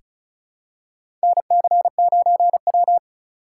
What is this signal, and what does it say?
Morse code, heard as a keyed tone, about 36 wpm, 700 Hz. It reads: NC9W
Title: K1USN SST QSO Practice - 35wpm